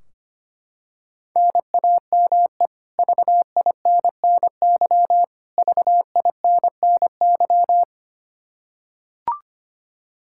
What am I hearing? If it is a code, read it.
Morse code, heard as a keyed tone, about 25 wpm, 700 Hz. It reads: NAME VINNY VINNY
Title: K1USN SST QSO Practice 25wpm